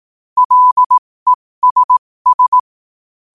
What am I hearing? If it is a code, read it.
Morse code, heard as a keyed tone, about 18 wpm, spaced as for 15 wpm, 1000 Hz. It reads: LESS
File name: qotdfiles / 15wpm